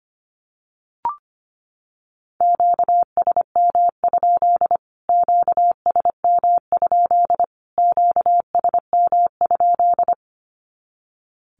Morse code, heard as a keyed tone, about 25 wpm, 700 Hz. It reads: QSM? QSM? QSM?
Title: Q-Codes Repeated 3x 25wpm